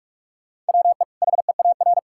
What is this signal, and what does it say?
Morse code, heard as a keyed tone, about 46 wpm, 700 Hz. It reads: WE HEAR
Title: Sentences from Top 1000 Words - Review Part 5 - 45wpm